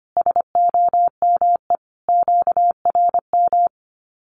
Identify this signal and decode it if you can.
Morse code, heard as a keyed tone, about 25 wpm, 700 Hz. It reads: SOME QRM
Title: Common QSO Phrases 25wpm